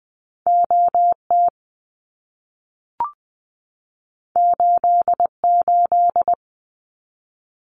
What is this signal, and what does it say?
Morse code, heard as a keyed tone, about 20 wpm, 700 Hz. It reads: OT 88
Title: Abbreviations 20wpm